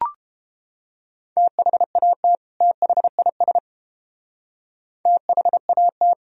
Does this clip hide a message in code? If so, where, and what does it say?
Yes: Morse code from 1.4 to 6.1 s, heard as a keyed tone, about 30 wpm, 700 Hz. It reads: THAT THIS THAT